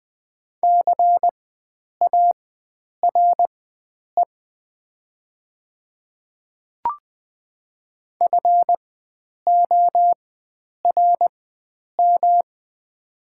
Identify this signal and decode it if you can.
Morse code, heard as a keyed tone, about 20 wpm, spaced as for 9 wpm, 700 Hz. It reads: CARE FORM
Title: 4x Character Spacing - Top 500 Words - 20wpm